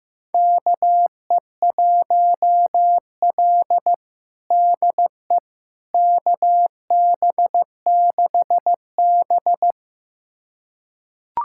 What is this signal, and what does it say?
Morse code, heard as a keyed tone, about 15 wpm, 700 Hz. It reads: KE1L DE KB6B
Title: USA Callsign DE Callsign - 4 Characters - 15wpm